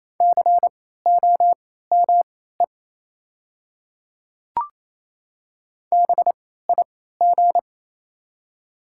Morse code, heard as a keyed tone, about 27 wpm, spaced as for 16 wpm, 700 Hz. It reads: COME BIG
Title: 3x Character Spacing - Top 500 Words - 28wpm